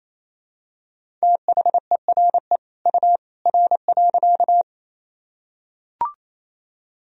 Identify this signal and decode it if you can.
Morse code, heard as a keyed tone, about 27 wpm, 700 Hz. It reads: THERE U R.
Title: Sentences from Top 100 Words and Abbreviations 28wpm